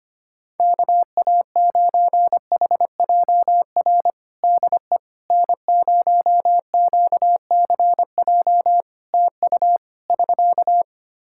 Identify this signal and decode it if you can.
Morse code, heard as a keyed tone, about 25 wpm, 700 Hz. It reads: KA9HJR DE N0QCJ TU <SK>
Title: Practice QSO - Exchange 4 - Ending - 25wpm